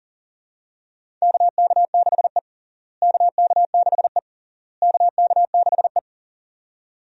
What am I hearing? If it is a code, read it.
Morse code, heard as a keyed tone, about 40 wpm, 700 Hz. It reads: KK6E KK6E KK6E
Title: USA Call Signs - 4 Characters - Repeated 3x - 40wpm